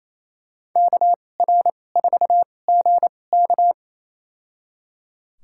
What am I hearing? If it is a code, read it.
Morse code, heard as a keyed tone, about 27 wpm, spaced as for 20 wpm, 700 Hz. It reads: KR4GK
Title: USA Call Signs - 5 Characters - 2x Character Spacing - 28wpm